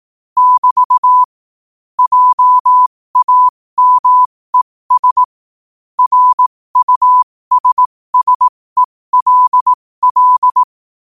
Morse code, heard as a keyed tone, about 18 wpm, spaced as for 15 wpm, 1000 Hz. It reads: <BT> JAMES RUSSELL
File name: qotdfiles / 15wpm